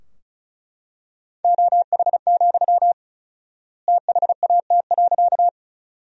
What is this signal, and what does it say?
Morse code, heard as a keyed tone, about 34 wpm, 700 Hz. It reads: OH, THAT.
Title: 4x Word-Spacing - Sentences from Top 500 Words - 35wpm